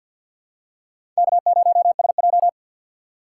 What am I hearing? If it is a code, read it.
Morse code, heard as a keyed tone, about 47 wpm, 700 Hz. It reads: K0SJ
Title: USA Call Signs - 4 Characters - 6x Character Spacing - 50wpm